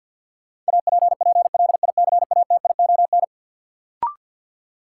Morse code, heard as a keyed tone, about 53 wpm, 700 Hz. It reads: APPLICATION
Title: Sentences from Top 1200 Words - New Word Drill - 50wpm